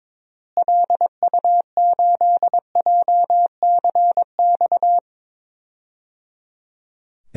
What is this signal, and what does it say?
Morse code, heard as a keyed tone, about 22 wpm, 700 Hz. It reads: LU8JCX
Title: DXCC-100 Call Signs - 6 Characters - Repeated 3x - 22wpm